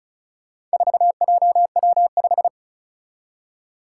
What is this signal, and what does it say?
Morse code, heard as a keyed tone, about 36 wpm, 700 Hz. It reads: 4JW5
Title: Four Challenging Letter-Number Combinations 35wpm